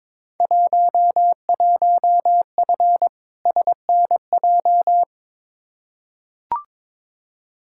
Morse code, heard as a keyed tone, about 22 wpm, 700 Hz. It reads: 11F SNJ
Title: Field Day Exchanges Encore - 22wpm